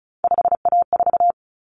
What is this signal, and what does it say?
Morse code, heard as a keyed tone, about 34 wpm, 700 Hz. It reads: 5A4
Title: Common Cut Numbers - 3-Digit 35wpm